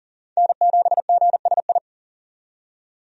Morse code, heard as a keyed tone, about 40 wpm, 700 Hz. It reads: N7GSI
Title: USA Call Signs - 5 Characters - 40wpm